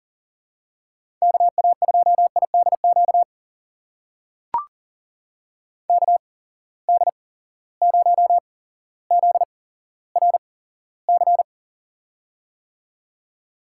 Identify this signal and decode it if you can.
Morse code, heard as a keyed tone, about 40 wpm, spaced as for 17 wpm, 700 Hz. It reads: KA2IDQ KD0ZRC
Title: USA Call Signs - 6 Characters - 8x Character Spacing - 40wpm